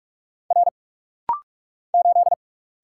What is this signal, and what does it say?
Morse code, heard as a keyed tone, about 46 wpm, 700 Hz. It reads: R 8
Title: Single Letter-Number - Rapid-Fire 45wpm